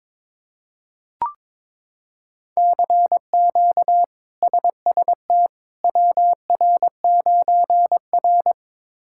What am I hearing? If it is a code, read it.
Morse code, heard as a keyed tone, about 21 wpm, 700 Hz. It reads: CQ SST WR9R
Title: K1USN SST QSO Practice - 22wpm